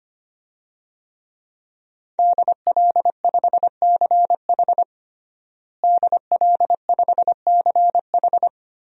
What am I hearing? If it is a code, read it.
Morse code, heard as a keyed tone, about 25 wpm, 700 Hz. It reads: DL5CH DL5CH
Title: DXCC-100 Call Signs - 5 Characters - Repeated 3x - 25wpm